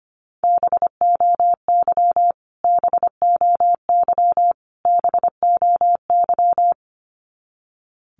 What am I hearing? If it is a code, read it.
Morse code, heard as a keyed tone, about 25 wpm, 700 Hz. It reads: BOY BOY BOY
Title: Top 200 Words Repeated 25wpm